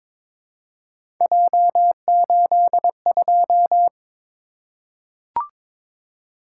Morse code, heard as a keyed tone, about 22 wpm, 700 Hz. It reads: J82
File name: practice / 22wpm / Three Letter-Number Combinations - 22wpm